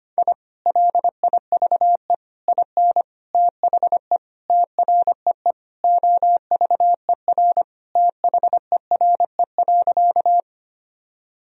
Morse code, heard as a keyed tone, about 25 wpm, 700 Hz. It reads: I LIVE IN THE TREE OVER THERE.
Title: Sentences from Top 1000 Words - Review Part 1 - 25wpm